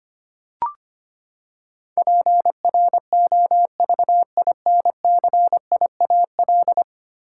Morse code, heard as a keyed tone, about 25 wpm, 700 Hz. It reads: PROVINCIAL
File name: Ten Letter Words - 25wpm